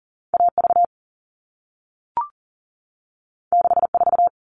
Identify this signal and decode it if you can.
Morse code, heard as a keyed tone, about 40 wpm, 700 Hz. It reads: AV 64
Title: All Cut Numbers - 2-Digit 40wpm